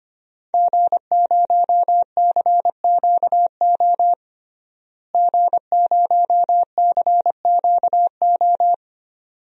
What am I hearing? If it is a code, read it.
Morse code, heard as a keyed tone, about 25 wpm, 700 Hz. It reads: G0CQO G0CQO
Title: DXCC-100 Call Signs - 5 Characters - Repeated 2x - 25wpm